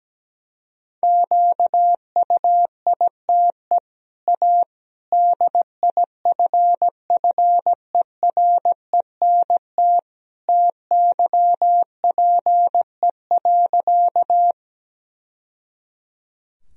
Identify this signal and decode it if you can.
Morse code, heard as a keyed tone, about 17 wpm, 700 Hz. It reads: QUITE A DIFFERENT TYPE.
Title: Sentences from Top 1000 Words - Review Part 3 - 17wpm